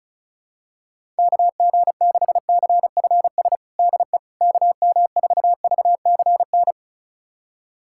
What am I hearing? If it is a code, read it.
Morse code, heard as a keyed tone, about 34 wpm, 700 Hz. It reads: KG6CFS DE KM4VCN
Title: USA Callsign DE Callsign - 6 Characters - 35wpm